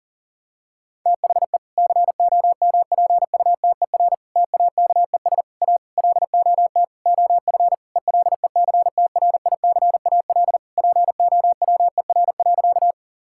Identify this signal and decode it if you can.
Morse code, heard as a keyed tone, about 40 wpm, 700 Hz. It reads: THE COMPUTER TAKES A LOT OF ELECTRICAL POWER.
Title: Sentences from Top 1000 Words - Review Part 6 - 40wpm